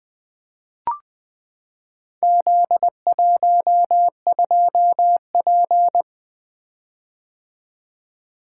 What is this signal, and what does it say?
Morse code, heard as a keyed tone, about 20 wpm, 700 Hz. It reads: Z12P
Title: Four Letter-Number Combinations - 20wpm